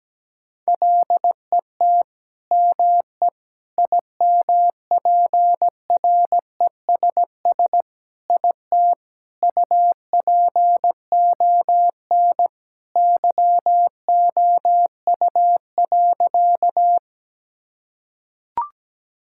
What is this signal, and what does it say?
Morse code, heard as a keyed tone, about 17 wpm, 700 Hz. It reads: LET ME IMPRESS IT UPON YOU.
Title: Sentences from Top 1600 Words 17wpm